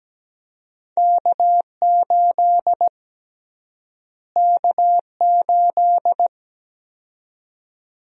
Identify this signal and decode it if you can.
Morse code, heard as a keyed tone, about 17 wpm, 700 Hz. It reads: K8 K8
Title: USA Call Sign Prefixes - Repeated 2x - 17wpm